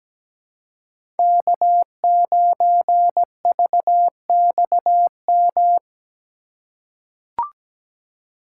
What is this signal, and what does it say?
Morse code, heard as a keyed tone, about 17 wpm, 700 Hz. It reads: K9VXM
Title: USA Call Signs - 5 Characters - 3x Character Spacing - 17wpm